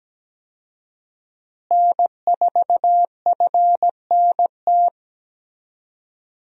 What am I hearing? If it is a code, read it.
Morse code, heard as a keyed tone, about 17 wpm, 700 Hz. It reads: N4FNT
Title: USA Call Signs - 5 Characters - Repeated 2x - 17wpm